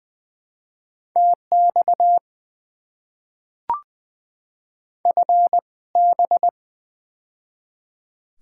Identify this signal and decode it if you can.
Morse code, heard as a keyed tone, about 20 wpm, spaced as for 17 wpm, 700 Hz. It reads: TX FB
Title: Top 90 QSO Elements - 2x Character Spacing - 20wpm